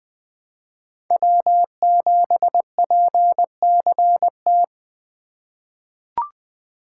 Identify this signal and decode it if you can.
Morse code, heard as a keyed tone, about 21 wpm, 700 Hz. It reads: W7PCT
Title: 2023 POTA Activator Call Signs - 20wpm